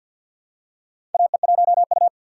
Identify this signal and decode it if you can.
Morse code, heard as a keyed tone, about 51 wpm, 700 Hz. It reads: AE1U
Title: USA Call Signs - 4 Characters - 6x Character Spacing - 50wpm